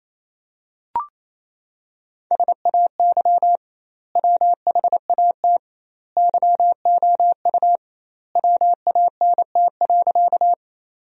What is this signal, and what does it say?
Morse code, heard as a keyed tone, about 28 wpm, 700 Hz. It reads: SAY WHAT YOU WANT.
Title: 2x Word-Spacing - Sentences from Top 200 Words - 28wpm